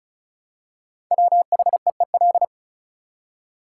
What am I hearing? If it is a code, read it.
Morse code, heard as a keyed tone, about 36 wpm, 700 Hz. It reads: WHEEL
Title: Five Letter Words - 35wpm